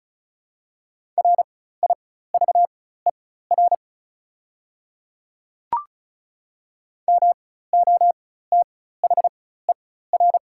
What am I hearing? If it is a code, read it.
Morse code, heard as a keyed tone, about 34 wpm, spaced as for 16 wpm, 690 Hz. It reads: RIVER MOTHER
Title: 4x Character Spacing - Top 300 Words - 35wpm